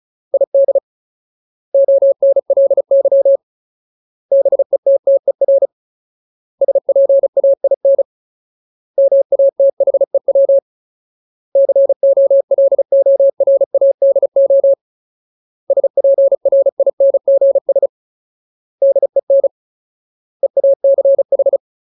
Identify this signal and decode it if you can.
Morse code, heard as a keyed tone, about 34 wpm, 550 Hz. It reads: ID ONLY BETTER SPAIN MATHEW COLORADO SPRINGS DEN EACH
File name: Ultimate Word List / 35WPM Ultimate